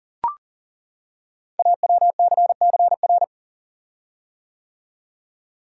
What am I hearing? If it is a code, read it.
Morse code, heard as a keyed tone, about 40 wpm, 690 Hz. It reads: AWCCR
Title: Code Groups - 5 Letters - 40wpm